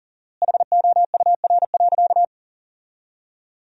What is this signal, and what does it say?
Morse code, heard as a keyed tone, about 37 wpm, 700 Hz. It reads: HOUR.
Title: 4x Word-Spacing - Sentences from Top 500 Words - 40wpm